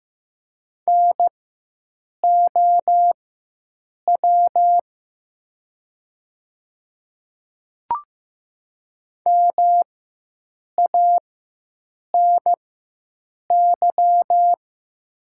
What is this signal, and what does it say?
Morse code, heard as a keyed tone, about 15 wpm, spaced as for 7 wpm, 700 Hz. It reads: NOW MANY
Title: 4x Character Spacing - Top 500 Words - 15wpm